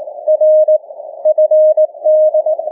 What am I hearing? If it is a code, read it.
Morse code, heard as a keyed tone, about 18 wpm, 620 Hz. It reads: R FD
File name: cw / ssn 30M